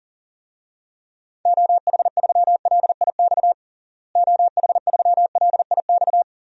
Morse code, heard as a keyed tone, about 40 wpm, 700 Hz. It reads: OH3LIX OH3LIX
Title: DXCC-100 Call Signs - 6 Characters - Repeated 2x - 40wpm